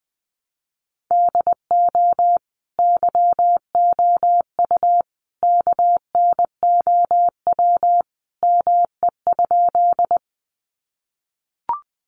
Morse code, heard as a keyed tone, about 20 wpm, 700 Hz. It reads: DO YOU KNOW ME?
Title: Sentences from Top 200 Words 20wpm